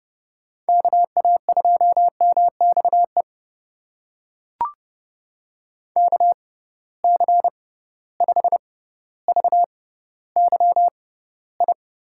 Morse code, heard as a keyed tone, about 30 wpm, spaced as for 15 wpm, 700 Hz. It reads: KA2MXE KC5VYI
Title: USA Call Signs - 6 Characters - 6x Character Spacing - 30wpm